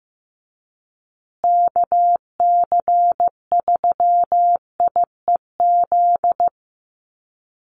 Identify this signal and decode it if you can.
Morse code, heard as a keyed tone, about 15 wpm, 700 Hz. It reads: KC3IEZ
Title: USA Call Signs Repeated 3x 15wpm